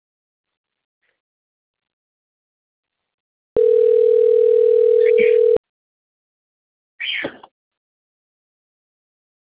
Ringback tone (North America), 440 + 480 Hz.